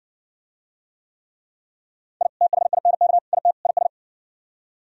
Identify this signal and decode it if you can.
Morse code, heard as a keyed tone, about 60 wpm, 700 Hz. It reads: I THINK IT IS